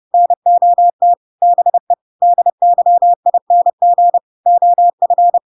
Morse code, heard as a keyed tone, about 30 wpm, 700 Hz. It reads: NOT BE DYING OF
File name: Pinocchio / Pinocchio ch5 30 30 ik1hge